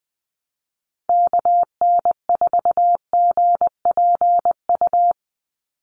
Morse code, heard as a keyed tone, about 20 wpm, 700 Hz. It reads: KN4GPU
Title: USA Call Signs 20wpm